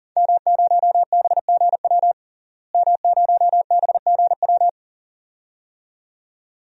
Morse code, heard as a keyed tone, about 40 wpm, 700 Hz. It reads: M0BGW M0BGW
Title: DXCC-100 Call Signs - 5 Characters - Repeated 2x - 40wpm